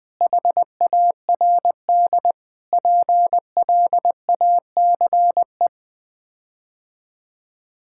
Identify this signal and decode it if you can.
Morse code, heard as a keyed tone, about 21 wpm, 700 Hz. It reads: HARD PLACE
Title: Sets of 8 Words - 20wpm